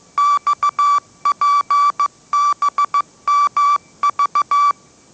Morse code, heard as a keyed tone, about 15 wpm, 1160 Hz. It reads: XPBMV
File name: mrx / mrx066 16wpm